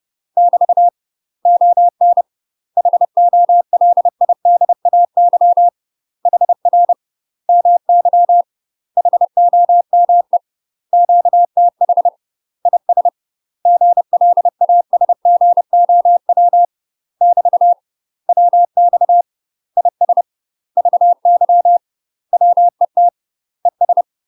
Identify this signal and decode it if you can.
Morse code, heard as a keyed tone, about 30 wpm, 700 Hz. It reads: <BT> ON HOLIDAY HR MY HOME QTH IS GLASGOW <BT> WX IS VY WET ES
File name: UK MASTER QSOs TRAINING/30WPM